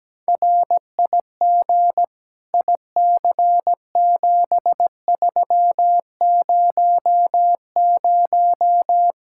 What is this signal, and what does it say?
Morse code, heard as a keyed tone, about 17 wpm, 700 Hz. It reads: RIG IC7300
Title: Practice QSO - Exchange 2 - Info - 17wpm